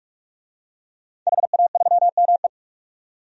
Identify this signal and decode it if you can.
Morse code, heard as a keyed tone, about 47 wpm, 700 Hz. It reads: HA3ME
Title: DXCC-100 Call Signs - 5 Characters - 45wpm